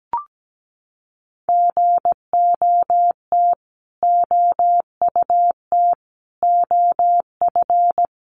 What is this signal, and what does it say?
Morse code, heard as a keyed tone, about 17 wpm, 700 Hz. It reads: GOT OUT OF
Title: Sets of 5 Words Encore 17wpm